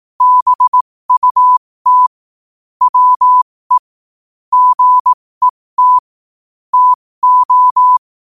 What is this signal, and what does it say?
Morse code, heard as a keyed tone, about 18 wpm, spaced as for 15 wpm, 1000 Hz. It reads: BUT WE GET TO
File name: qotdfiles / 15wpm